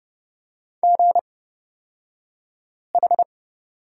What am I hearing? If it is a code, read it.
Morse code, heard as a keyed tone, about 29 wpm, spaced as for 5 wpm, 690 Hz. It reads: GH
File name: Lesson 36d - BT - Two Characters 30wpm